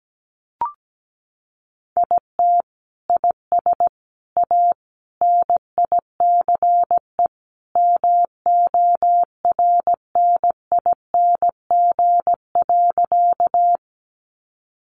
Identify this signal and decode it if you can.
Morse code, heard as a keyed tone, about 17 wpm, 700 Hz. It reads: IT IS A NICE MORNING.
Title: Sentences from Top 1100 Words 17wpm